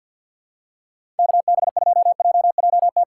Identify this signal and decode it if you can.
Morse code, heard as a keyed tone, about 47 wpm, 700 Hz. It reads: KB2JJT
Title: USA Call Signs - 6 Characters - 6x Character Spacing - 50wpm